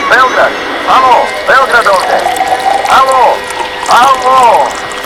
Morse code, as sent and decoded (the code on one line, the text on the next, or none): none